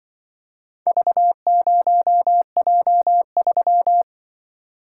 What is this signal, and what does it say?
Morse code, heard as a keyed tone, about 24 wpm, 700 Hz. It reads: V0J3
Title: Speed-Racing - Four Letter-Number Combinations - 20wpm